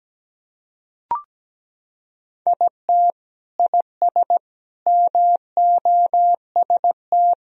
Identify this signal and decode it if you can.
Morse code, heard as a keyed tone, about 17 wpm, 700 Hz. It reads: IT IS MOST
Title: Sentences from Top 1000 Words - Review Part 1 - 17wpm